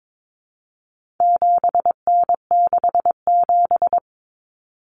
Morse code, heard as a keyed tone, about 22 wpm, 700 Hz. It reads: 7N67
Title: Common Cut Numbers - 4-Digit 22wpm